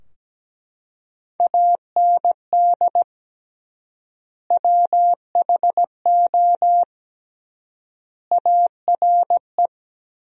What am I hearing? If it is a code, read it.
Morse code, heard as a keyed tone, about 17 wpm, 700 Hz. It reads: AND WHO ARE
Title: 3x Word-Spacing - Sentences from Top 100 Words - 17wpm